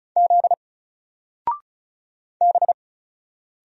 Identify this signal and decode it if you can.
Morse code, heard as a keyed tone, about 33 wpm, 700 Hz. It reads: Z B
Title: Single Letter-Number 35wpm